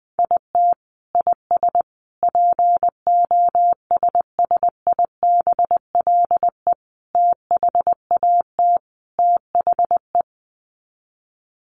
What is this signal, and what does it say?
Morse code, heard as a keyed tone, about 20 wpm, 700 Hz. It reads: IT IS POSSIBLE THAT THE
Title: Sets of 5 Words Encore2 20wpm